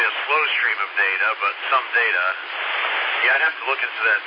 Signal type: Emergency Alert System (SAME) tones